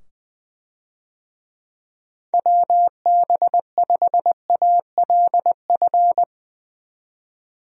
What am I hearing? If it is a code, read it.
Morse code, heard as a keyed tone, about 20 wpm, 700 Hz. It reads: WB5ALF
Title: USA Call Signs - 6 Characters - 20wpm